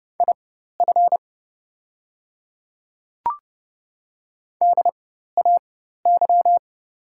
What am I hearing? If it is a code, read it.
Morse code, heard as a keyed tone, about 30 wpm, spaced as for 14 wpm, 700 Hz. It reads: IF DAY
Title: 4x Character Spacing - Top 100 Words - 30wpm